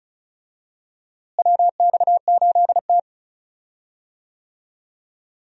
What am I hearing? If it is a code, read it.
Morse code, heard as a keyed tone, about 36 wpm, 700 Hz. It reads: WX8T
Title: USA Call Signs - 4 Characters - Repeated 3x - 35wpm